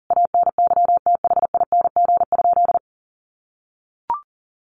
Morse code, heard as a keyed tone, about 40 wpm, 700 Hz. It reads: ANYTHING?
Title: Sentences from Top 1500 Words 40wpm